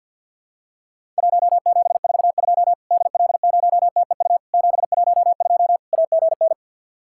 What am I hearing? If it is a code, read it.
Morse code, heard as a keyed tone, about 50 wpm, 700 Hz. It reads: 1742 DL0TEU 712 AGN
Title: Worked All Europe DX Contest - QTC Exchange with Repeats - 50wpm